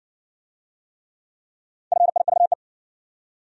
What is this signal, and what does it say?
Morse code, heard as a keyed tone, about 56 wpm, 700 Hz. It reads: FIVE